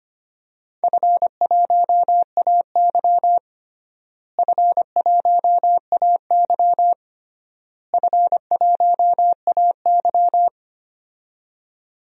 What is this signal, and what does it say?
Morse code, heard as a keyed tone, about 25 wpm, 700 Hz. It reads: F1AY F1AY F1AY
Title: DXCC-100 Call Signs - 4 Characters - Repeated 3x - 25wpm